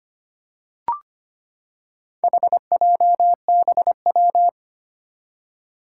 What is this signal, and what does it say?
Morse code, heard as a keyed tone, about 25 wpm, 700 Hz. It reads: HJBW
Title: Four Challenging Letter-Number Combinations 25wpm